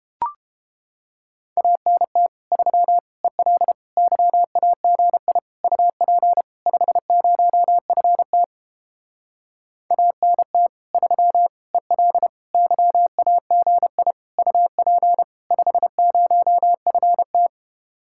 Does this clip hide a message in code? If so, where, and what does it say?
Yes: Morse code from 1.6 to 17.5 s, heard as a keyed tone, about 29 wpm, 700 Hz. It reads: ANT 3 EL YAGI UP 50FT ANT 3 EL YAGI UP 50FT